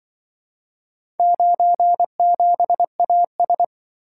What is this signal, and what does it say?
Morse code, heard as a keyed tone, about 25 wpm, 700 Hz. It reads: 97AS
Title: Speed-Racing - Four Letter-Number Combinations - 20wpm